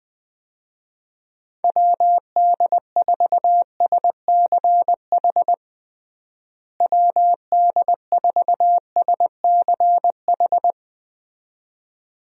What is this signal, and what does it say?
Morse code, heard as a keyed tone, about 20 wpm, 700 Hz. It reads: WD4SCH WD4SCH
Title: USA Call Signs - 6 Characters - Repeated 2x - 20wpm